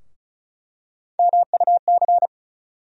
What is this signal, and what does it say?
Morse code, heard as a keyed tone, about 34 wpm, 700 Hz. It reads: MUC
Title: USA Call Sign Postfixes - 3 Letters 35wpm